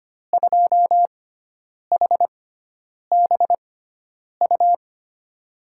Morse code, heard as a keyed tone, about 25 wpm, spaced as for 9 wpm, 700 Hz. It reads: 2HBU
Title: USA Call Signs - 6 Characters - 6x Character Spacing - 25wpm